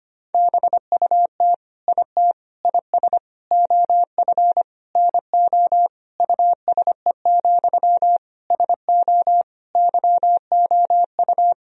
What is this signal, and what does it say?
Morse code, heard as a keyed tone, about 25 wpm, 700 Hz. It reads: BUT IT IS OF NO USE, SO YOU
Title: Sentences from Top 1000 Words - Review Part 2 - 25wpm